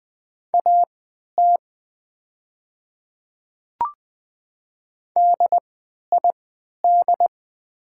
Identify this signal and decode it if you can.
Morse code, heard as a keyed tone, about 20 wpm, spaced as for 11 wpm, 700 Hz. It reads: AT DID
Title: 3x Character Spacing - Top 100 Words - 20wpm